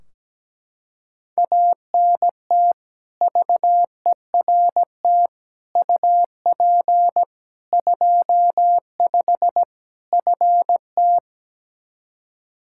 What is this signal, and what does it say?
Morse code, heard as a keyed tone, about 17 wpm, 700 Hz. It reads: ANT VERT UP 25 FT
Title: Practice QSO - Exchange 2 - Info - 17wpm